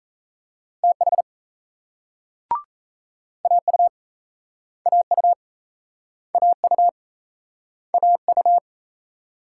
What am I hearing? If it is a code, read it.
Morse code, heard as a keyed tone, about 40 wpm, 700 Hz. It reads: TH AU AU AU AU